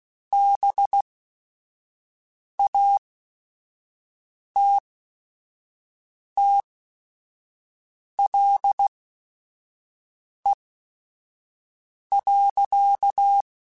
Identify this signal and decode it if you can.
Morse code, heard as a keyed tone, about 16 wpm, spaced as for 5 wpm, 800 Hz. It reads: BATTLE.